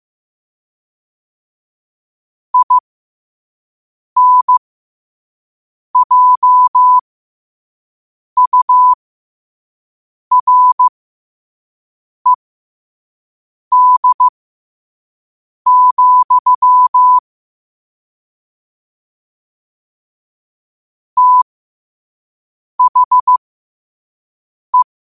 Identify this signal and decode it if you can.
Morse code, heard as a keyed tone, about 15 wpm, spaced as for 5 wpm, 1000 Hz. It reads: INJURED, THE
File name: qotdfiles / 5wpm